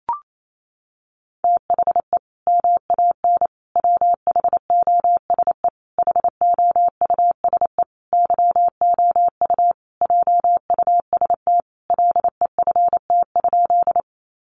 Morse code, heard as a keyed tone, about 28 wpm, 700 Hz. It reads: THE MAN WHOSE HOUSE YOU JUST LEFT?
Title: Sentences from Top 1200 Words 28wpm